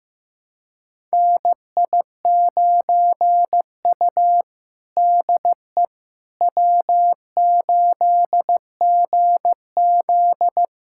Morse code, heard as a keyed tone, about 15 wpm, 700 Hz. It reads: NI9U DE W8GZ
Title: USA Callsign DE Callsign - 4 Characters - 15wpm